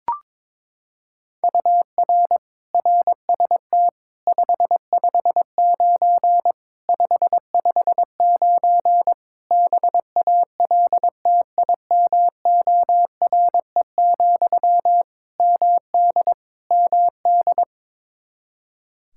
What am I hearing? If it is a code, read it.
Morse code, heard as a keyed tone, about 22 wpm, 700 Hz. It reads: UR RST 559 559 BALTIMORE, MD MD
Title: Practice QSO - Exchange 1 - Beginning - 22wpm